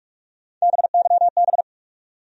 Morse code, heard as a keyed tone, about 43 wpm, 700 Hz. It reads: BYB
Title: USA Call Sign Postfixes - 3 Letters 45wpm